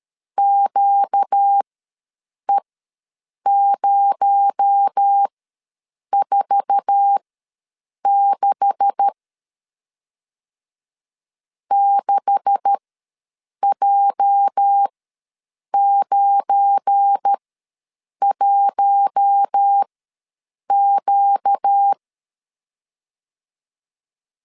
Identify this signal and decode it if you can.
Morse code, heard as a keyed tone, about 13 wpm, spaced as for 7 wpm, 800 Hz. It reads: QE046 6J91Q
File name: LEZIONI / 9 286 ws 12wpm 3351